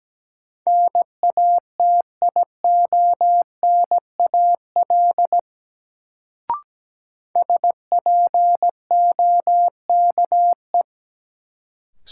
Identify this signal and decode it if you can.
Morse code, heard as a keyed tone, about 17 wpm, 700 Hz. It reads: NATIONAL SPOKE
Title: Sentences from Top 1200 Words - New Word Drill Alt - 17wpm